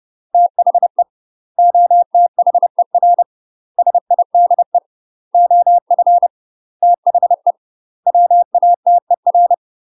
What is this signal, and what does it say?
Morse code, heard as a keyed tone, about 30 wpm, 700 Hz. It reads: THE OTHER SIDE OF THE WATER
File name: UK MASTER QSOs TRAINING / 30WPM / UK MASTER QSOs 0001